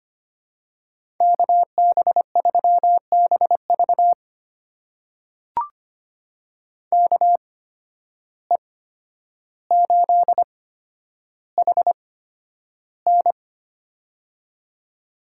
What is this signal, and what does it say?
Morse code, heard as a keyed tone, about 25 wpm, spaced as for 11 wpm, 700 Hz. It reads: KB3BV KE8HN
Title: USA Call Signs - 5 Characters - 8x Character Spacing - 25wpm